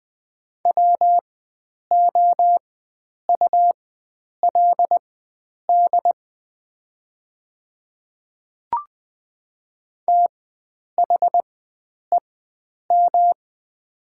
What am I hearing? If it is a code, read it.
Morse code, heard as a keyed tone, about 20 wpm, spaced as for 9 wpm, 700 Hz. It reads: WOULD THEM
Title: 4x Character Spacing - Top 100 Words - 20wpm